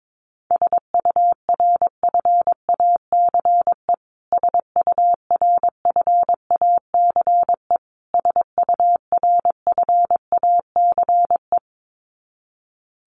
Morse code, heard as a keyed tone, about 22 wpm, 700 Hz. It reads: SURFACE SURFACE SURFACE
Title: Top 500 Words Repeated 22wpm